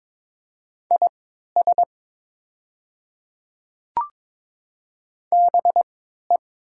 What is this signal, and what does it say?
Morse code, heard as a keyed tone, about 22 wpm, spaced as for 12 wpm, 700 Hz. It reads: IS BE
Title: 3x Character Spacing - Top 100 Words - 22wpm